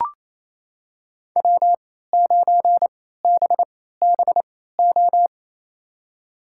Morse code, heard as a keyed tone, about 28 wpm, spaced as for 16 wpm, 700 Hz. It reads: W9BBO
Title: USA Call Signs - 3x Character Spacing - 28wpm